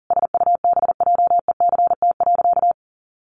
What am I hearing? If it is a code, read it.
Morse code, heard as a keyed tone, about 40 wpm, 700 Hz. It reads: SUBJECT.
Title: Sentences from Top 1400 Words 40wpm